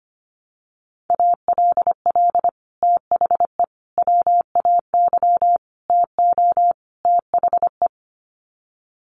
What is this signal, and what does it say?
Morse code, heard as a keyed tone, about 25 wpm, 700 Hz. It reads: ALL THE WAY TO THE
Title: Sets of 5 Words 25wpm